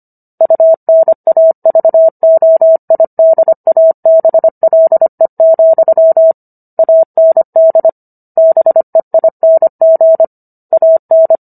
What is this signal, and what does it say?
Morse code, heard as a keyed tone, about 25 wpm, 650 Hz. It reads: UNAVOIDABLE, AND BEING AN